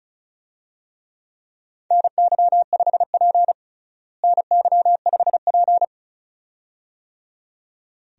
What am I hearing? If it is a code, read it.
Morse code, heard as a keyed tone, about 36 wpm, 700 Hz. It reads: NY5P NY5P
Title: USA Call Signs - 4 Characters - Repeated 2x - 35wpm